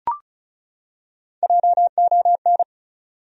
Morse code, heard as a keyed tone, about 35 wpm, 700 Hz. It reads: JON
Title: USA Common Names - Max 4 Letters - 35wpm